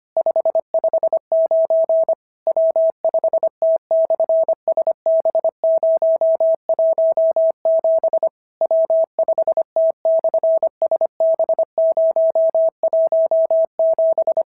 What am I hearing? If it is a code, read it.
Morse code, heard as a keyed tone, about 25 wpm, 650 Hz. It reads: 559 W5T/SB017 W5T/SB017
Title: SOTA QSO Practice 25wpm